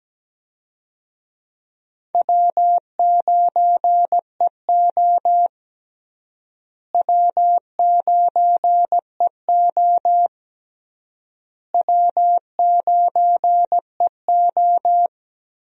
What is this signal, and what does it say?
Morse code, heard as a keyed tone, about 17 wpm, 700 Hz. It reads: W9EO W9EO W9EO
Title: USA Call Signs - 4 Characters - Repeated 3x - 17wpm